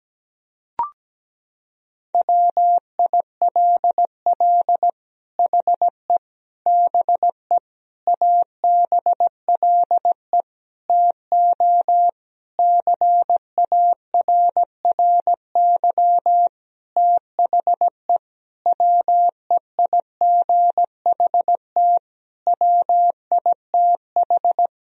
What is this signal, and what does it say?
Morse code, heard as a keyed tone, about 17 wpm, 700 Hz. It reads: WILL HE BE ABLE TO CARRY THE WEIGHT WITH
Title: Sentences from Top 1000 Words - Review Part 3 - 17wpm